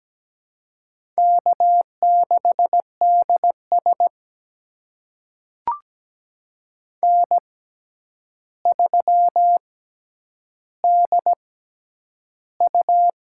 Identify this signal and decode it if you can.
Morse code, heard as a keyed tone, about 17 wpm, spaced as for 9 wpm, 700 Hz. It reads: K6DS N3DU
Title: USA Call Signs - 6x Character Spacing - 17wpm